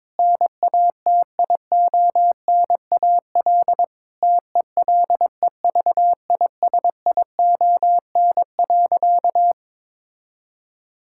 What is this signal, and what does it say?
Morse code, heard as a keyed tone, about 22 wpm, 700 Hz. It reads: NATIONAL TELEVISION.
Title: Sentences from Top 1000 Words - Review Part 5 - 22wpm